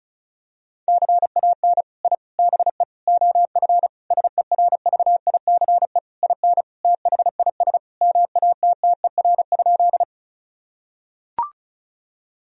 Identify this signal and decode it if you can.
Morse code, heard as a keyed tone, about 34 wpm, 700 Hz. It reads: CAN I BE OF SERVICE IN THIS MATTER?
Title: Sentences from Top 1000 Words - Review Part 4 - 35wpm